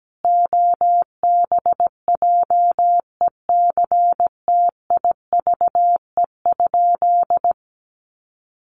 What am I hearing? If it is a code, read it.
Morse code, heard as a keyed tone, about 17 wpm, 700 Hz. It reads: OBJECTIVE?
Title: Sentences from Top 1200 Words 17wpm